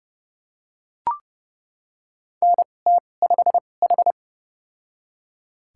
Morse code, heard as a keyed tone, about 30 wpm, spaced as for 22 wpm, 700 Hz. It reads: NT5H
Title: USA Call Signs - 4 Characters - 2x Character Spacing - 30wpm